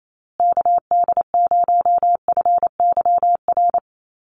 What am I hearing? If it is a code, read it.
Morse code, heard as a keyed tone, about 28 wpm, 700 Hz. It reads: KD0FYR
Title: USA Call Signs Repeated 2x 28wpm